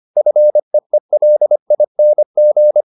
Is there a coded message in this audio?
Yes: Morse code from 0.2 to 2.8 s, heard as a keyed tone, about 25 wpm, 600 Hz. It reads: FEELING